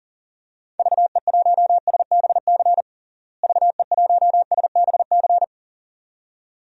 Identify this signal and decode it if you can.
Morse code, heard as a keyed tone, about 40 wpm, 700 Hz. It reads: VE1SBC VE1SBC
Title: DXCC-100 Call Signs - 6 Characters - Repeated 2x - 40wpm